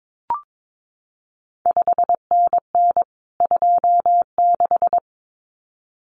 Morse code, heard as a keyed tone, about 22 wpm, 700 Hz. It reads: 5NN 26
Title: CQ WW DX Contest Exchange Encore - 22wpm